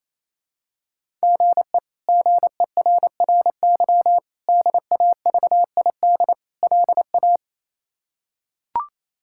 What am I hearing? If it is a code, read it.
Morse code, heard as a keyed tone, about 28 wpm, 700 Hz. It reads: GE GERRY DAVID LA
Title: K1USN SST QSO Practice - 28wpm